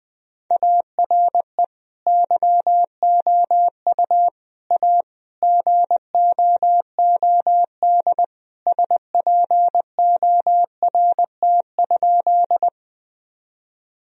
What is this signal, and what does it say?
Morse code, heard as a keyed tone, about 21 wpm, 700 Hz. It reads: ARE YOU A GOOD SPORT?
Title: Sentences from Top 1000 Words - Review Part 4 - 20wpm